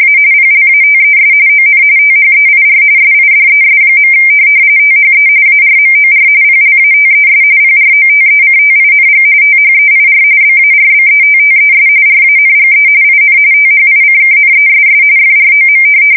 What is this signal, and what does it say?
RTTY, about 45.45 baud. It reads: A PLATES ON THEIR SIDES. THE ARMOR PLATES PROTECTING THEIR BARBETTES RANGE IN THICKNESS FROM
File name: fsk170 / USS-Missouri